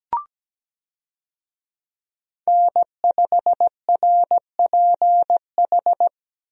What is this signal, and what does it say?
Morse code, heard as a keyed tone, about 17 wpm, 700 Hz. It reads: N5RPH
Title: USA Call Signs - 5 Characters - Repeated 2x - 17wpm